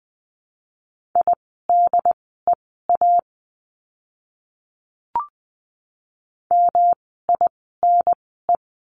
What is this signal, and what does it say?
Morse code, heard as a keyed tone, about 20 wpm, spaced as for 14 wpm, 700 Hz. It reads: IDEA MINE
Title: 108 Lesson 18f - Letter m - Words 20wpm